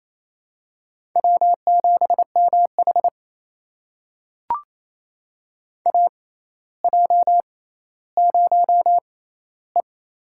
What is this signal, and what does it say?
Morse code, heard as a keyed tone, about 28 wpm, spaced as for 14 wpm, 700 Hz. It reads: W7MH AJ0E
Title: USA Call Signs - 4 Characters - 6x Character Spacing - 28wpm